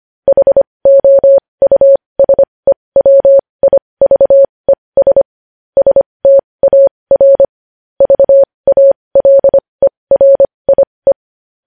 Morse code, heard as a keyed tone, about 25 wpm, spaced as for 20 wpm, 550 Hz. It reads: HOUSEWIVES STAR VALERIE
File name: podcast / News headlines-1x-25x20